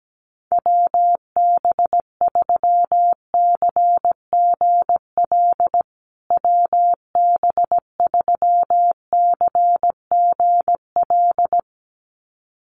Morse code, heard as a keyed tone, about 17 wpm, 700 Hz. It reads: WB3CGL WB3CGL
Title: USA Call Signs Repeated 2x 17wpm